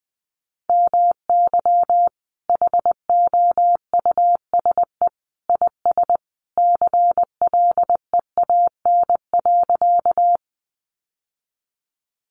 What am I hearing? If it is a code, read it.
Morse code, heard as a keyed tone, about 19 wpm, 700 Hz. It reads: MY HOUSE IS CLEAN.
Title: Sentences from Top 1100 Words 20wpm